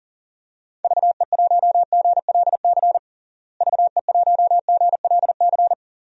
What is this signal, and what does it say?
Morse code, heard as a keyed tone, about 40 wpm, 700 Hz. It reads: VE1GLC VE1GLC
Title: DXCC-100 Call Signs - 6 Characters - Repeated 2x - 40wpm